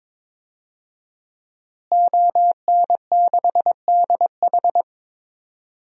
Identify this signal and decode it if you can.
Morse code, heard as a keyed tone, about 22 wpm, 700 Hz. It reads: ON6DH
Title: DXCC-100 Call Signs - 5 Characters - Repeated 3x - 22wpm